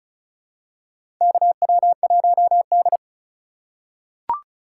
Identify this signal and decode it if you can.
Morse code, heard as a keyed tone, about 36 wpm, 700 Hz. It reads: KW1D
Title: USA Call Signs - 4 Characters - 4x Character Spacing - 35wpm